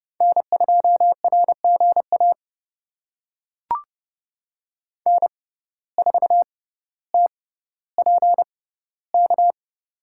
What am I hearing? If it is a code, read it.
Morse code, heard as a keyed tone, about 30 wpm, spaced as for 15 wpm, 700 Hz. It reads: N2RGA N4TPK
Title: USA Call Signs - 5 Characters - 6x Character Spacing - 30wpm